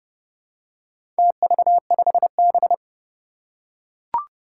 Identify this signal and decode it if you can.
Morse code, heard as a keyed tone, about 32 wpm, 700 Hz. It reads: TV5B
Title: Four Challenging Letter-Number Combinations 30wpm